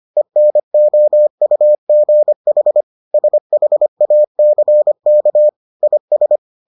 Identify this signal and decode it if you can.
Morse code, heard as a keyed tone, about 25 wpm, 600 Hz. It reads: ENOUGH SHACK IS